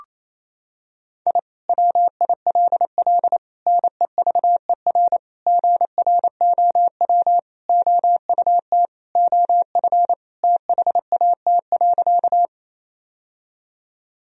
Morse code, heard as a keyed tone, about 28 wpm, 700 Hz. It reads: I WILL NEVER GROW OUT OF THAT.
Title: Sentences from Top 1000 Words - Review Part 1 - 28wpm